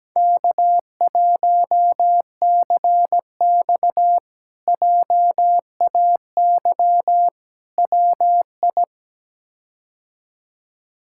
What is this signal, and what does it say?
Morse code, heard as a keyed tone, about 17 wpm, 700 Hz. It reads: K1CX JAY WI
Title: K1USN SST QSO Practice - 17wpm